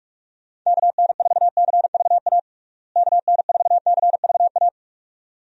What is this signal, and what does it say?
Morse code, heard as a keyed tone, about 42 wpm, 700 Hz. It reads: KN4CVA KN4CVA
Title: USA Call Signs - 6 Characters - Repeated 2x - 45wpm